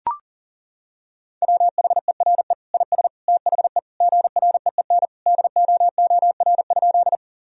Morse code, heard as a keyed tone, about 40 wpm, 700 Hz. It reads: WHERE IS THE GREEN DOOR?
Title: Sentences from Top 1000 Words - Review Part 2 - 40wpm